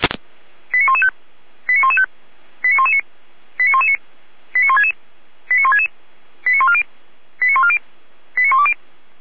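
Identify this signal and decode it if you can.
Five-tone selective call (ZVEI), heard as quick runs of five tones, tones of about 70 ms. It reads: 89115 89115 89119 89119 89166 89166 89140 89140 89120
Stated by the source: Received on an SDR as a ZVEI five-tone call.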